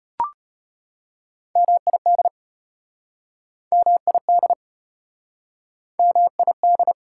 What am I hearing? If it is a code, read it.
Morse code, heard as a keyed tone, about 34 wpm, 690 Hz. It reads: MID MID MID